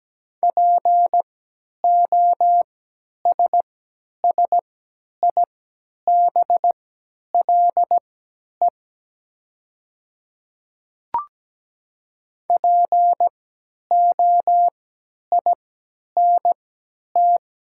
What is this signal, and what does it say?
Morse code, heard as a keyed tone, about 17 wpm, spaced as for 10 wpm, 700 Hz. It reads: POSSIBLE POINT
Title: 3x Character Spacing - Top 500 Words - 17wpm